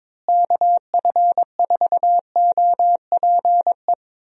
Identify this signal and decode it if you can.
Morse code, heard as a keyed tone, about 23 wpm, 700 Hz. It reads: KF4OPE
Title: USA Call Signs - 6 Characters - Repeated 2x - 22wpm